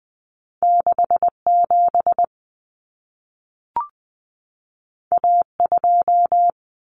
Morse code, heard as a keyed tone, about 20 wpm, 700 Hz. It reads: 67 A2
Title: Common Cut Numbers - 2-Digit 20wpm